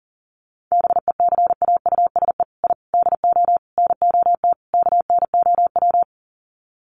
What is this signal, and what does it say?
Morse code, heard as a keyed tone, about 40 wpm, 700 Hz. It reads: BECAUSE I DO NOT KNOW
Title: Sets of 5 Words Encore2 40wpm